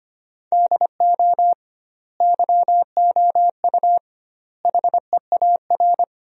Morse code, heard as a keyed tone, about 25 wpm, 700 Hz. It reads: DO YOU HEAR
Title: 2x Word-Spacing - Sentences from Top 300 Words - 25wpm